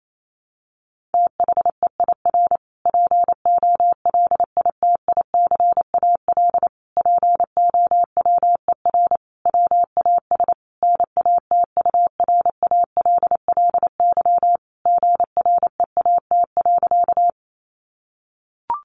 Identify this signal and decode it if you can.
Morse code, heard as a keyed tone, about 28 wpm, 700 Hz. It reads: THEIR POLITICAL POWER WAS NATURALLY GREAT.
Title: Sentences from Top 1300 Words 28wpm